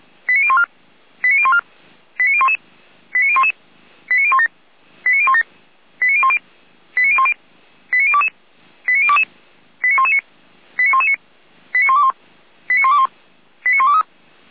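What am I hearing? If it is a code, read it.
Five-tone selective call (ZVEI), heard as quick runs of five tones, tones of about 70 ms. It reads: 89014 89014 89011 89011 89017 89017 89010 89010 89022 89022 89119 89119 89121 89121 89123
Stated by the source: ZVEI five-tone call received with an SDR.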